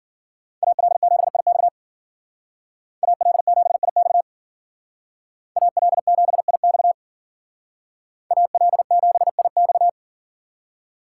Morse code, heard as a keyed tone, about 50 wpm, 700 Hz. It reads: AL7IX AL7IX AL7IX AL7IX
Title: Speed Racing - USA Call Signs 40wpm